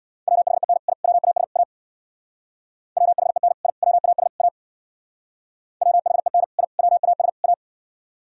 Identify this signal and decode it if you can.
Morse code, heard as a keyed tone, about 59 wpm, 690 Hz. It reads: WHEN I WAS A WHEN I WAS A WHEN I WAS A
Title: Speed-Racing - Sets of 4 Words 50wpm